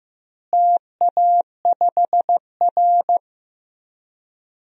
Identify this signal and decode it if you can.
Morse code, heard as a keyed tone, about 15 wpm, 700 Hz. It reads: TA5R
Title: DXCC-100 Call Signs - 4 Characters - Repeated 3x - 15wpm